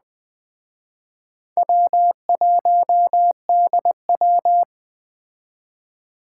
Morse code, heard as a keyed tone, about 20 wpm, 700 Hz. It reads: W1DW
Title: Speed Racing - USA 1x2 Call Signs 17wpm